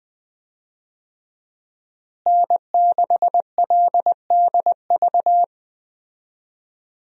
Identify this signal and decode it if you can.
Morse code, heard as a keyed tone, about 21 wpm, 700 Hz. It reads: N6LDV
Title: USA Call Signs - 5 Characters - Repeated 2x - 20wpm